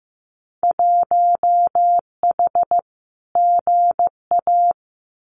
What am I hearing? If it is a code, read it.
Morse code, heard as a keyed tone, about 15 wpm, 700 Hz. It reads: 1H GA
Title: Winter Field Day Exchanges 15wpm